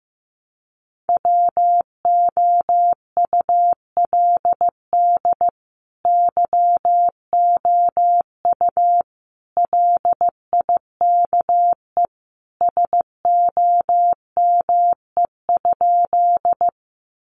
Morse code, heard as a keyed tone, about 15 wpm, 700 Hz. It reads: WOULD YOU LIKE SOME?
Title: Sentences from Top 100 Words 15wpm